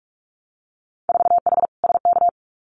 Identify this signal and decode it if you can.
Morse code, heard as a keyed tone, about 46 wpm, 690 Hz. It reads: 4H SK
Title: Winter Field Day Exchanges Encore 45wpm